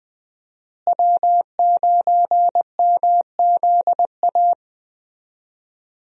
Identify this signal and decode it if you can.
Morse code, heard as a keyed tone, about 20 wpm, 700 Hz. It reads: W9MZA
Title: USA Call Signs - 5 Characters - Repeated 2x - 20wpm